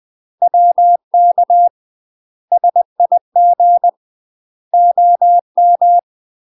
Morse code, heard as a keyed tone, about 20 wpm, 700 Hz. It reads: WK SIG OM
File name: UK MASTER QSOs TRAINING / 20WPM / UK MASTER QSOs 0014